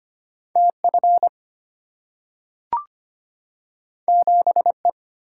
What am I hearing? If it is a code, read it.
Morse code, heard as a keyed tone, about 25 wpm, 700 Hz. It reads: TF 7E
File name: Two Letter-Number Combinations - 25wpm